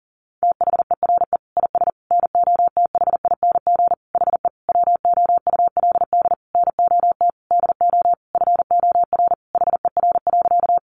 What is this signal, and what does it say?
Morse code, heard as a keyed tone, about 40 wpm, 700 Hz. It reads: THERE IS NOTHING HE WOULD NOT DO FOR HER.
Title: Sentences from Top 600 Words 40wpm